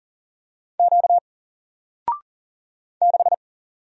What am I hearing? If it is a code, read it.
Morse code, heard as a keyed tone, about 40 wpm, 700 Hz. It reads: Q 6
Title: Single Letter-Number 40wpm